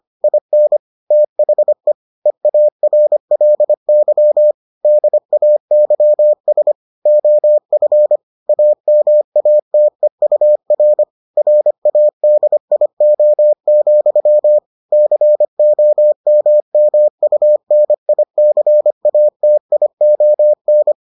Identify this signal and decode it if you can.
Morse code, heard as a keyed tone, about 25 wpm, 600 Hz. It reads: IN THE EARLY DAYS OF AMATEUR RADIO, COMMUNICATION